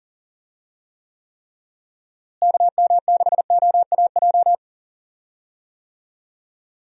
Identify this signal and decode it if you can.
Morse code, heard as a keyed tone, about 40 wpm, 700 Hz. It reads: KM6OAJ
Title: USA Call Signs - 6 Characters - Repeated 3x - 40wpm